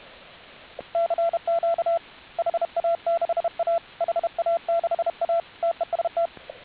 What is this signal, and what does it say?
Morse code, heard as a keyed tone, about 32 wpm, 680 Hz, with radio noise. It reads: CQ HA6A HA6A TEST